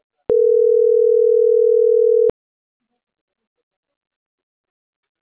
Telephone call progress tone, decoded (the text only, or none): ringback tone